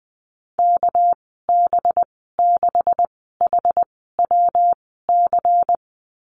Morse code, heard as a keyed tone, about 20 wpm, spaced as for 15 wpm, 700 Hz. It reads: KB6HWC
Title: 213 Lesson 34g - Number 8 - Call Signs 20wpm